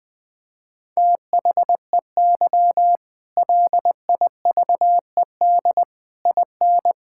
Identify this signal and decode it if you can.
Morse code, heard as a keyed tone, about 21 wpm, 700 Hz. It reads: THEY LIVED IN
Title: Sentences with 8-Word Phrases - 20wpm